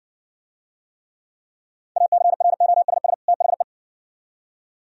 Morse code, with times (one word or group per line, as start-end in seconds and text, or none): 2.0-3.1 ACROSS
3.3-3.6 THE